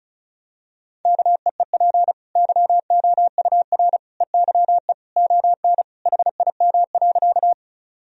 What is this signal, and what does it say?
Morse code, heard as a keyed tone, about 36 wpm, 700 Hz. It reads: KEEP YOUR EYE ON HIM.
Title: Sentences from Top 1000 Words - Review Part 1 - 35wpm